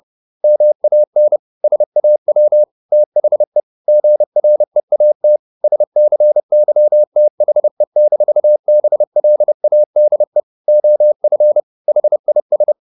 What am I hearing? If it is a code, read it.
Morse code, heard as a keyed tone, about 30 wpm, 600 Hz. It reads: MAN SAW THE GREAT SCYTHE-BLADE OF HIS
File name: the old man and the sea / the old man and the sea 1 30wpm